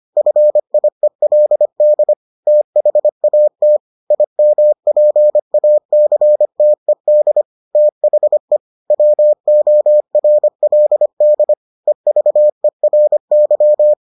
Morse code, heard as a keyed tone, about 25 wpm, 600 Hz. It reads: FIELD THAT IMPACTED THE WORLD EVERY